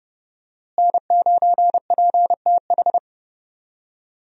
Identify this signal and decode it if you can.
Morse code, heard as a keyed tone, about 32 wpm, 700 Hz. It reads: N9PTH
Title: USA Call Signs - 5 Characters - Repeated 2x - 30wpm